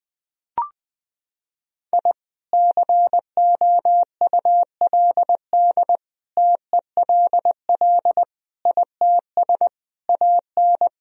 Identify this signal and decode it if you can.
Morse code, heard as a keyed tone, about 20 wpm, 700 Hz. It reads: I COULD TELL ITS AN
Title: Sentences from Top 1000 Words - Review Part 2 - 20wpm